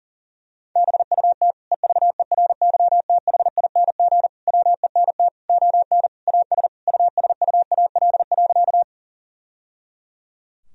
Morse code, heard as a keyed tone, about 37 wpm, 700 Hz. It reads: BUT EVERYTHING WENT ON AS USUAL.
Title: Sentences from Top 1000 Words - Review Part 2 - 40wpm